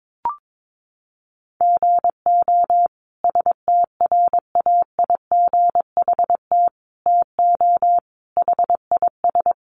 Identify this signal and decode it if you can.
Morse code, heard as a keyed tone, about 22 wpm, 700 Hz. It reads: GO STRAIGHT TO HIS
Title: Sentences from Top 1000 Words 22wpm